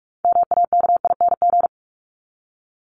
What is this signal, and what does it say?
Morse code, heard as a keyed tone, about 44 wpm, 700 Hz. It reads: MAKING
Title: ING Suffix - Top 100 Words with Base Words 45wpm